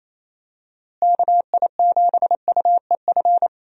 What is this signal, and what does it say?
Morse code, heard as a keyed tone, about 27 wpm, 700 Hz. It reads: KI7UEF
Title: USA Call Signs - 6 Characters - Repeated 3x - 28wpm